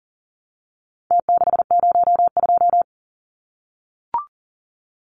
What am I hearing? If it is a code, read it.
Morse code, heard as a keyed tone, about 40 wpm, 700 Hz. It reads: T602
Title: Numbers - Contest Style - 4-Digit 40wpm